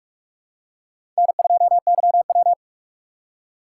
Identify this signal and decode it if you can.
Morse code, heard as a keyed tone, about 45 wpm, 700 Hz. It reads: N2YW
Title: USA Call Signs - 4 Characters - 2x Character Spacing - 45wpm